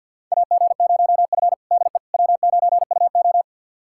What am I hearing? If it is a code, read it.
Morse code, heard as a keyed tone, about 50 wpm, 700 Hz. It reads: AG0F DE W9UO
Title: USA Callsign DE Callsign - 4 Characters - 50wpm